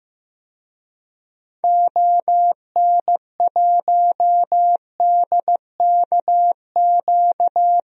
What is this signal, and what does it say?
Morse code, heard as a keyed tone, about 15 wpm, 700 Hz. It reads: ON1DKQ
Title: DXCC-100 Call Signs - 6 Characters - 15wpm